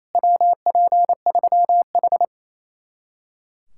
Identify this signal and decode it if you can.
Morse code, heard as a keyed tone, about 37 wpm, 700 Hz. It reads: WP3H